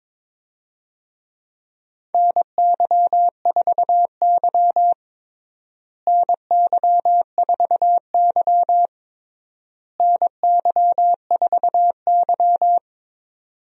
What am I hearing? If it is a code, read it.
Morse code, heard as a keyed tone, about 22 wpm, 700 Hz. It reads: NY4Y NY4Y NY4Y
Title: USA Call Signs - 4 Characters - Repeated 3x - 22wpm